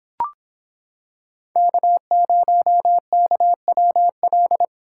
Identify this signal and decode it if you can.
Morse code, heard as a keyed tone, about 26 wpm, 700 Hz. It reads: K0KWL